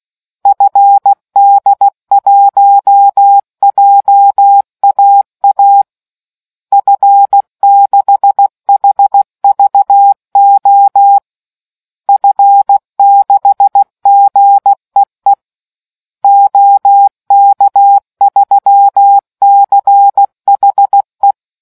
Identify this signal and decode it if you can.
Morse code, heard as a keyed tone, about 16 wpm, 800 Hz. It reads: FD1JAA F6HVO F6GEE OK3CHE